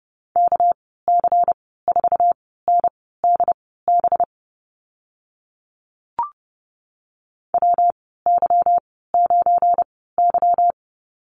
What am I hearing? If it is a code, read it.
Morse code, heard as a keyed tone, about 30 wpm, spaced as for 17 wpm, 700 Hz. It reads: KC4NDB WY9Y
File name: Lesson 37g - Letter x - Call Signs 30wpm